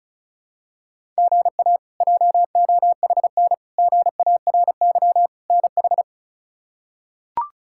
Morse code, heard as a keyed tone, about 36 wpm, 700 Hz. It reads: GA JOHN GARY NH
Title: K1USN SST QSO Practice - 35wpm